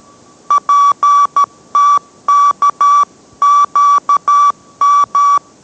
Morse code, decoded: PTKQM

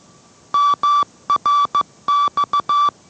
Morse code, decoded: MRX

